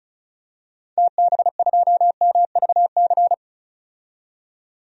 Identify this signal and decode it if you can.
Morse code, heard as a keyed tone, about 34 wpm, 700 Hz. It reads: TB2MVC
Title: DXCC-100 Call Signs - 6 Characters - 35wpm